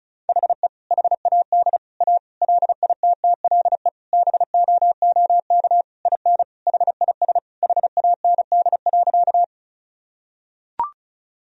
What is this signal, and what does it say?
Morse code, heard as a keyed tone, about 36 wpm, 700 Hz. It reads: HE HAD A LITTLE BOOK IN HIS HAND.
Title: Sentences from Top 1000 Words - Review Part 1 - 35wpm